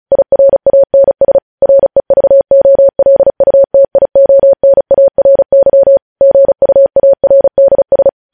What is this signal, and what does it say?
Morse code, heard as a keyed tone, about 36 wpm, 550 Hz. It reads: IRANS REVOLUTIONARY GUARDS
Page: news headlines in morse code at 35 wpm